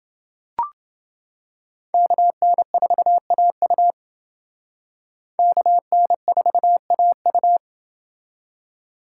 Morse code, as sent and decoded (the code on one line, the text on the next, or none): -.- -. ....- .- ..- / -.- -. ....- .- ..-
KN4AU KN4AU